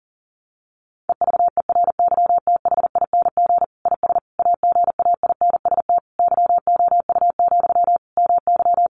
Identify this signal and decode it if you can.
Morse code, heard as a keyed tone, about 40 wpm, 700 Hz. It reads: EVERYTHING IS AGAINST YOU, MY
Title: Sentences from Top 700 Words 40wpm